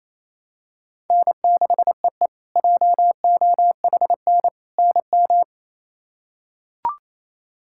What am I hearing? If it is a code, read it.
Morse code, heard as a keyed tone, about 27 wpm, 700 Hz. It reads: N6EE JOHN NM
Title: K1USN SST QSO Practice - 28wpm